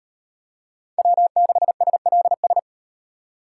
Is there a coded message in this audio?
Yes: Morse code from 1.0 to 2.6 s, heard as a keyed tone, about 38 wpm, 700 Hz. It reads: W6SLS